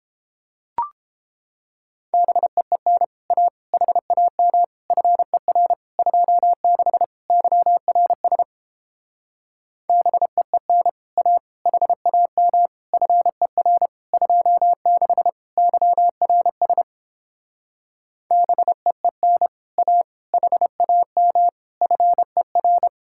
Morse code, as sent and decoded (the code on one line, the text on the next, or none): -... . -. / .- / .... .- -- / ..-. .-. / ..--- -.... / -.-- .-. ... / -... . -. / .- / .... .- -- / ..-. .-. / ..--- -.... / -.-- .-. ... / -... . -. / .- / .... .- -- / ..-. .-.
BEEN A HAM FER 26 YRS BEEN A HAM FER 26 YRS BEEN A HAM FER